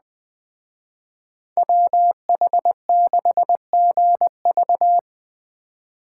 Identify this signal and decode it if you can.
Morse code, heard as a keyed tone, about 21 wpm, 700 Hz. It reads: WH6GV
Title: Speed Racing - USA 2x2 Call Signs 17wpm